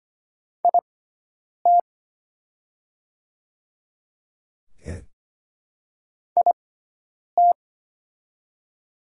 Morse code, heard as a keyed tone, about 25 wpm, spaced as for 9 wpm, 690 Hz. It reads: IT IT
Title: 6x Character Spacing - Top 200 Words - 25wpm